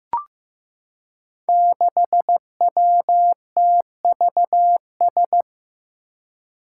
Morse code, heard as a keyed tone, about 15 wpm, 700 Hz. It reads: 6WTVS
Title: Five Challenging Letter-Number Combinations 15wpm